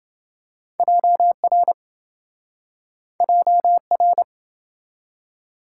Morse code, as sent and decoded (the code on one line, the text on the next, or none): .--- .-. / .--- .-.
JR JR